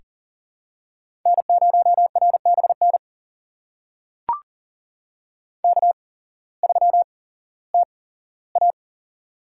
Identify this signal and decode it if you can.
Morse code, heard as a keyed tone, about 40 wpm, spaced as for 19 wpm, 700 Hz. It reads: N0RBN K3TA
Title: USA Call Signs - 8x Character Spacing - 40wpm